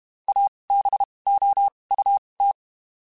Morse code, heard as a keyed tone, about 32 wpm, spaced as for 23 wpm, 800 Hz. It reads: ABOUT
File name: QSOs / QSOs-mp3s / WD0BGP 1@30